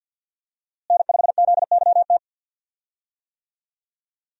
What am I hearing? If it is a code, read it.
Morse code, heard as a keyed tone, about 52 wpm, 700 Hz. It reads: N5ZYT
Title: USA Call Signs - 5 Characters - Repeated 2x - 50wpm